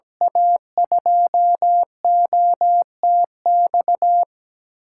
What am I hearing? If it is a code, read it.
Morse code, heard as a keyed tone, about 17 wpm, 700 Hz. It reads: A2OTX
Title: USA Call Signs - 6 Characters - 8x Character Spacing - 17wpm